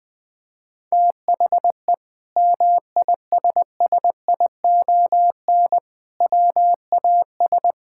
Morse code, heard as a keyed tone, about 20 wpm, 700 Hz. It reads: THE MISSION WAS
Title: Sentences from Top 1600 Words 20wpm